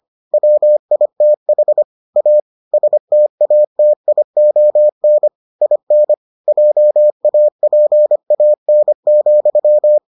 Morse code, heard as a keyed tone, about 25 wpm, 600 Hz. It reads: WITH A STATION IN JAPAN,